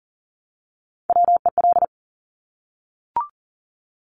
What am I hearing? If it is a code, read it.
Morse code, heard as a keyed tone, about 40 wpm, 700 Hz. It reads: WEL
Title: Top Three Letter Combinations in Words 40wpm